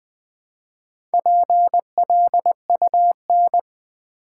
Morse code, heard as a keyed tone, about 20 wpm, 700 Hz. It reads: PLUN
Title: Build A Word - 2 Letters At A Time - 6-Letter Words - 20wpm